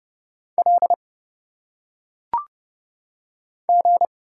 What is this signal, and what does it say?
Morse code, heard as a keyed tone, about 30 wpm, 700 Hz. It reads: L G
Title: Single Letters - Phonetic Pronunciation 30wpm